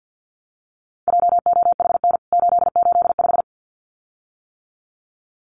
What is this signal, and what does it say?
Morse code, heard as a keyed tone, about 50 wpm, 700 Hz. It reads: JOHN 885
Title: CWT Exchange 50wpm